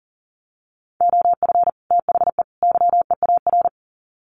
Morse code, heard as a keyed tone, about 40 wpm, 700 Hz. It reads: OF THE YEAR
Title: Sets of 3 Words 40wpm